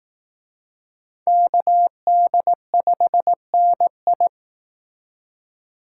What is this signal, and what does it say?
Morse code, heard as a keyed tone, about 19 wpm, 700 Hz. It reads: KD5NI